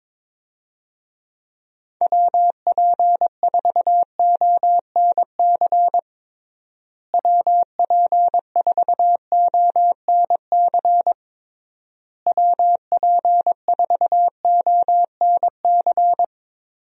Morse code, heard as a keyed tone, about 22 wpm, 700 Hz. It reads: WP4ONC WP4ONC WP4ONC
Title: USA Call Signs - 6 Characters - Repeated 3x - 22wpm